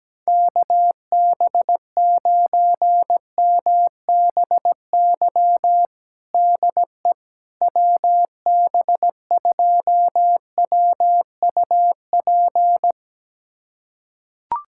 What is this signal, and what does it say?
Morse code, heard as a keyed tone, about 17 wpm, 700 Hz. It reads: KB9MBY DE WB2WUP
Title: USA Callsign DE Callsign - 6 Characters - 17wpm